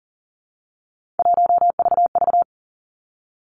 Morse code, heard as a keyed tone, about 43 wpm, 700 Hz. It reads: 1VV
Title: Three Challenging Letter-Number Combinations 40wpm